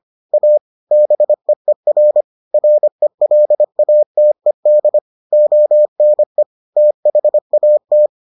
Morse code, heard as a keyed tone, about 25 wpm, 600 Hz. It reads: A BEER RELATED ONE THAT